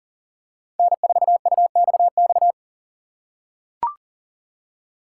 Morse code, heard as a keyed tone, about 40 wpm, 700 Hz. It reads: N4UXX
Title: USA Call Signs - 5 Characters - 6x Character Spacing - 40wpm